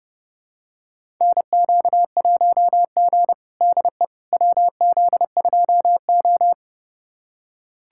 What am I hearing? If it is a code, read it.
Morse code, heard as a keyed tone, about 32 wpm, 700 Hz. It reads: NQ1G DE WZ2O
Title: USA Callsign DE Callsign - 4 Characters - 30wpm